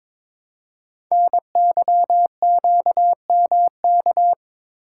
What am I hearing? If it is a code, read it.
Morse code, heard as a keyed tone, about 23 wpm, 700 Hz. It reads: NYQMK
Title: Code Groups - 5 Letters - 22wpm